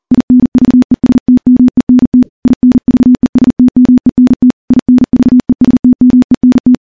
Morse code, heard as a keyed tone, about 40 wpm, 250 Hz. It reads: INVESTMENT INVESTMENT INVESTMENT
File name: morse-training-40WPM